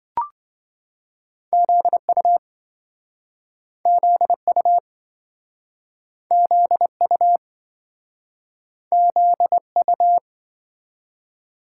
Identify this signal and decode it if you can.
Morse code, heard as a keyed tone, about 27 wpm, 700 Hz. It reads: ZU ZU ZU ZU